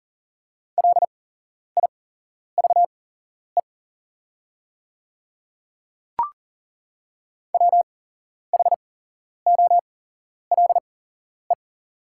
Morse code, heard as a keyed tone, about 40 wpm, spaced as for 11 wpm, 700 Hz. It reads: LIVE WHOLE
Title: 8x Character Spacing - Top 500 Words - 40wpm